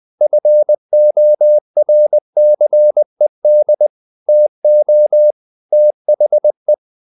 Morse code, heard as a keyed tone, about 20 wpm, 600 Hz. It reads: FORCED TO THE
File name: the old man and the sea / the old man and the sea part 1 20wpm